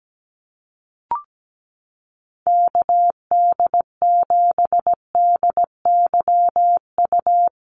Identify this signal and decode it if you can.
Morse code, heard as a keyed tone, about 17 wpm, 700 Hz. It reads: KD7DYU
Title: USA Call Signs Repeated 3x 17wpm